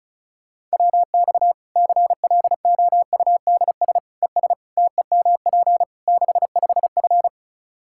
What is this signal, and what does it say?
Morse code, heard as a keyed tone, about 36 wpm, 700 Hz. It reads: WX CLOUDS ES TEMP 65F
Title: Practice QSO - Exchange 2 - Info - 35wpm